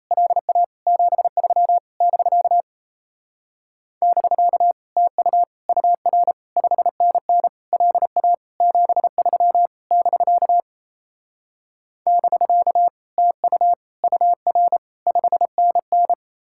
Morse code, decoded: LA 73 <BK> <BK> TU UR 5NN LA 73 <BK> <BK> TU UR 5NN

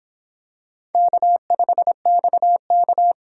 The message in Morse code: K5XK

